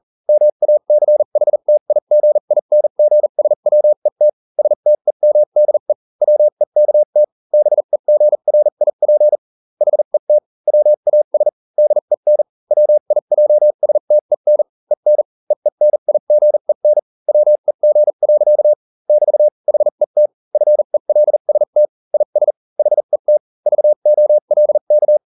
Morse code, heard as a keyed tone, about 40 wpm, spaced as for 35 wpm, 600 Hz. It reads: MACHTIGINGSWET STEMDE WEKT BEGRIP HET WAS DEN WIJSTEN EN EENIGEN WEG. <BT> HET FELST IS HET VOLK